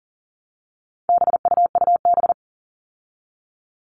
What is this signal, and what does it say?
Morse code, heard as a keyed tone, about 40 wpm, 700 Hz. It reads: BUUB
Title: All Cut Numbers - 4-Digit 40wpm